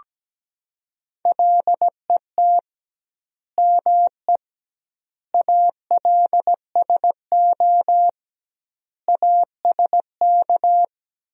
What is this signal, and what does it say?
Morse code, heard as a keyed tone, about 17 wpm, 700 Hz. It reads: LET ME ALSO ASK
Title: 2x Word-Spacing - Sentences from Top 400 Words - 17wpm